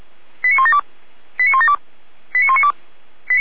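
Five-tone selective call (ZVEI), heard as quick runs of five tones, tones of about 70 ms. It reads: 89172 89172 89192
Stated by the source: ZVEI five-tone selective call 89327 received by SDR.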